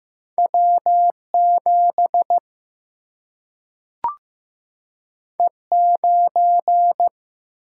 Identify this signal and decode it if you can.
Morse code, heard as a keyed tone, about 15 wpm, 700 Hz. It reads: W 7 E 9
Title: Single-Letter Single-Number 15wpm